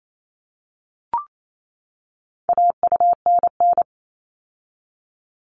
Morse code, heard as a keyed tone, about 28 wpm, 700 Hz. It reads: AUNN
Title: All Cut Numbers - 4-Digit 28wpm